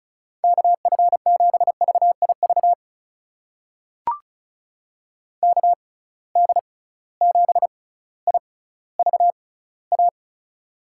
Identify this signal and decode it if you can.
Morse code, heard as a keyed tone, about 36 wpm, spaced as for 18 wpm, 700 Hz. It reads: KF7VIV KD7IVA
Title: USA Call Signs - 6 Characters - 6x Character Spacing - 35wpm